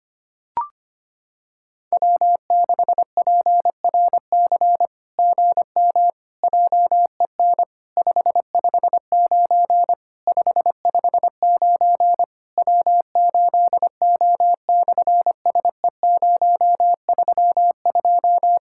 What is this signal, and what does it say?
Morse code, heard as a keyed tone, about 25 wpm, 700 Hz. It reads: W6PRC GM JEN 559 559 W8O/SE032
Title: SOTA QSO Practice 25wpm